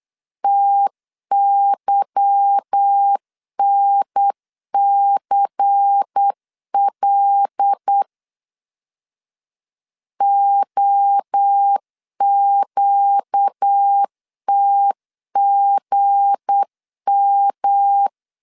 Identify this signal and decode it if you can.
Morse code, heard as a keyed tone, about 8 wpm, 800 Hz. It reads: TYNCL OQTGM
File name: LEZIONI / 8 YJ45 / 8 YJ4 8wpm 3246